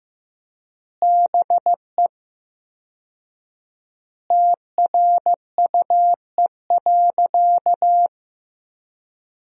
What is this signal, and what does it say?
Morse code, heard as a keyed tone, about 15 wpm, 700 Hz. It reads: BE TRUE.
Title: 4x Word-Spacing - Sentences from Top 400 Words - 15wpm